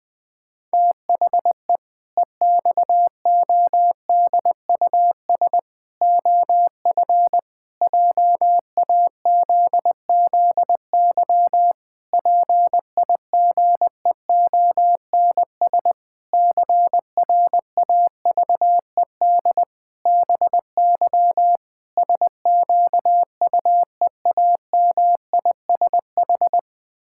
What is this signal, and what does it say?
Morse code, heard as a keyed tone, about 19 wpm, 700 Hz. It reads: THE EXODUS OF JAZZY PIGEONS CRAVED BY SQUEAMISH
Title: Pangram Challenge v2 - 20wpm